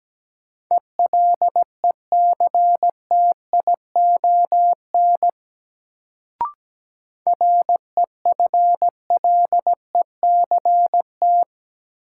Morse code, heard as a keyed tone, about 17 wpm, 700 Hz. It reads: ELECTION REFLECT
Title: Sentences from Top 1500 Words - New Word Drill Alt - 17wpm